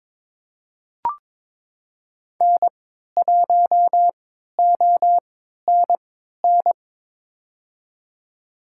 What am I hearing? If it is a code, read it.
Morse code, heard as a keyed tone, about 22 wpm, spaced as for 12 wpm, 700 Hz. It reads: N1ONN
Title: USA Call Signs - 5 Characters - 3x Character Spacing - 22wpm